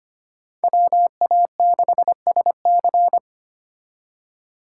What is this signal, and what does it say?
Morse code, heard as a keyed tone, about 32 wpm, 700 Hz. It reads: WA6SC